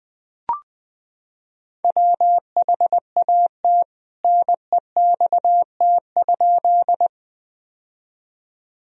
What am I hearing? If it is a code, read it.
Morse code, heard as a keyed tone, about 19 wpm, 700 Hz. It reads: WHAT NEXT?
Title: Sentences from Top 1000 Words - Review Part 1 - 20wpm